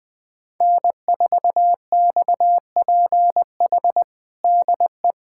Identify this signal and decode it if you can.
Morse code, heard as a keyed tone, about 20 wpm, 700 Hz. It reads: N4XPH DE
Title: USA Callsign DE Callsign - 5 Characters - 20wpm